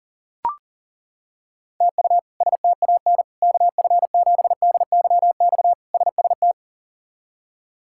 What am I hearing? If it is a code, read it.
Morse code, heard as a keyed tone, about 40 wpm, 700 Hz. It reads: TU STAN KF7DYX SST
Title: K1USN SST QSO Practice - 40wpm